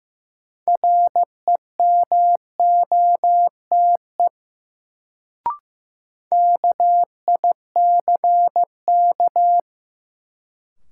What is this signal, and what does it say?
Morse code, heard as a keyed tone, about 15 wpm, 700 Hz. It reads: REMOTE KICK
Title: Sentences from Top 1300 Words - New Word Drill Alt - 15wpm